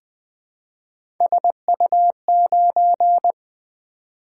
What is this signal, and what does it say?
Morse code, heard as a keyed tone, about 20 wpm, 700 Hz. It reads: SU9
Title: Three Letter-Number Combinations - 20wpm